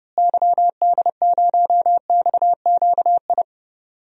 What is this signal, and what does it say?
Morse code, heard as a keyed tone, about 30 wpm, 700 Hz. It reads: YD0XQI
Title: DXCC-100 Call Signs - 6 Characters - Repeated 2x - 30wpm